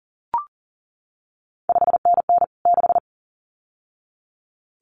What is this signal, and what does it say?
Morse code, heard as a keyed tone, about 38 wpm, 700 Hz. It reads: 5NN 6
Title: CQ WW DX Contest Exchange Encore - 40wpm